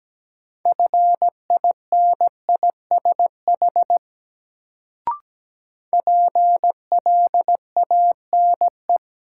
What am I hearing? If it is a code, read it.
Morse code, heard as a keyed tone, about 17 wpm, 700 Hz. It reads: FINISH PLANE
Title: Sentences from Top 1000 Words - Review Part 3 - New Word Drill - 17wpm